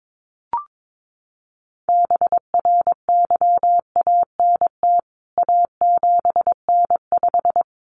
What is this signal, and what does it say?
Morse code, heard as a keyed tone, about 22 wpm, 700 Hz. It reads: BRYANT A7N5
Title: CWT Exchange 22wpm